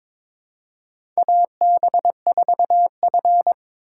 Morse code, heard as a keyed tone, about 22 wpm, 700 Hz. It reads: AB4F
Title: USA Call Signs - 4 Characters - 3x Character Spacing - 22wpm